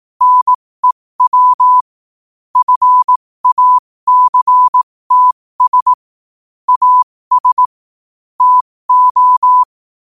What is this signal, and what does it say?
Morse code, heard as a keyed tone, about 18 wpm, spaced as for 15 wpm, 1000 Hz. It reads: NEW FACTS AS TO
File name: qotdfiles / 15wpm